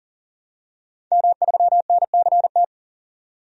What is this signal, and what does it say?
Morse code, heard as a keyed tone, about 40 wpm, 700 Hz. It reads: M3NCT
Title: DXCC-100 Call Signs - 5 Characters - 40wpm